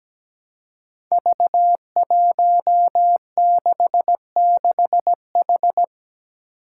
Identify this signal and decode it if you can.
Morse code, heard as a keyed tone, about 17 wpm, 700 Hz. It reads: V166H
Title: Five Challenging Letter-Number Combinations 17wpm